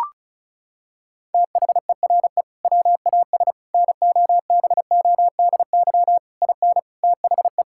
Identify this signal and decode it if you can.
Morse code, heard as a keyed tone, about 36 wpm, 700 Hz. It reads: THERE WAS NOBODY IN THE
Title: Sentences from Top 1600 Words 35wpm